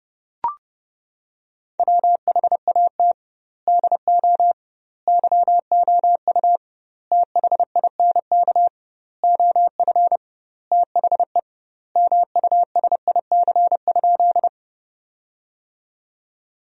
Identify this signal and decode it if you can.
Morse code, heard as a keyed tone, about 30 wpm, 700 Hz. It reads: WHAT DO YOU THINK OF THE MUSIC?
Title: 2x Word-Spacing - Sentences from Top 300 Words - 30wpm